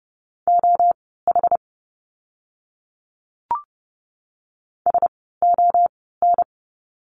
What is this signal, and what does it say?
Morse code, heard as a keyed tone, about 30 wpm, spaced as for 17 wpm, 700 Hz. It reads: OH SON
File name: Lesson 16f - Letter c - Words 30wpm